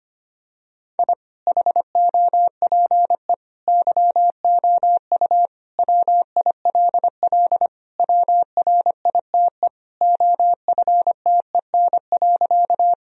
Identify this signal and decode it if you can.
Morse code, heard as a keyed tone, about 25 wpm, 700 Hz. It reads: I HOPE YOU WILL WRITE OFTEN.
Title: Sentences from Top 1000 Words - Review Part 1 - 25wpm